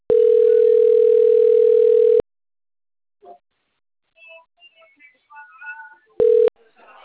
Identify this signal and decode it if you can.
Ringback tone (North America), 440 + 480 Hz, with speech.
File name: out- - - - - .1437